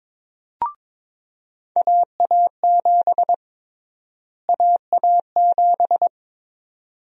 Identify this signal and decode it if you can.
Morse code, heard as a keyed tone, about 22 wpm, 700 Hz. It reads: AA7 AA7
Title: USA Call Sign Prefixes - Repeated 2x - 22wpm